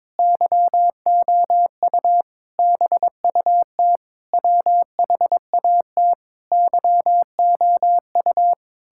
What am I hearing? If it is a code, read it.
Morse code, heard as a keyed tone, about 22 wpm, 700 Hz. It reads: YOU BUT WHAT YOU
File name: Sentences with 9-Word Phrases - 22wpm